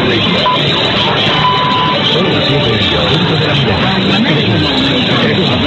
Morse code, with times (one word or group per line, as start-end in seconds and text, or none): none